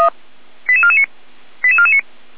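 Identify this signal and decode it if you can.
Five-tone selective call (ZVEI), heard as quick runs of five tones, tones of about 70 ms. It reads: 88449 88449
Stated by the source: Received on an SDR, ZVEI five-tone selective call, from the Feuerwehr Orlamünde (fire brigade) audio archive.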